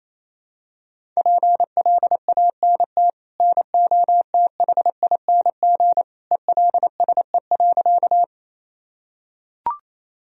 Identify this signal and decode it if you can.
Morse code, heard as a keyed tone, about 28 wpm, 700 Hz. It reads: PLANT NOTHING ELSE.
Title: Sentences from Top 1000 Words - Review Part 1 - 28wpm